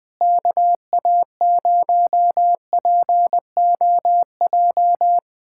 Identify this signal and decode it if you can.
Morse code, heard as a keyed tone, about 21 wpm, 700 Hz. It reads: KA0POJ
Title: USA Callsign DE Callsign - 6 Characters - 20wpm